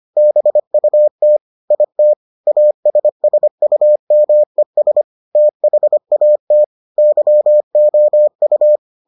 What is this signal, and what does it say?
Morse code, heard as a keyed tone, about 25 wpm, 600 Hz. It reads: BUT IT ASSUMES THAT YOU